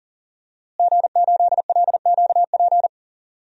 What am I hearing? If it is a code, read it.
Morse code, heard as a keyed tone, about 40 wpm, 700 Hz. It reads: G8LQP
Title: DXCC-100 Call Signs - 5 Characters - Repeated 2x - 40wpm